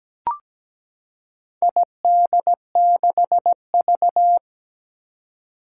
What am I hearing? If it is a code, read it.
Morse code, heard as a keyed tone, about 17 wpm, 700 Hz. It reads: ID6V
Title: Four Challenging Letter-Number Combinations 17wpm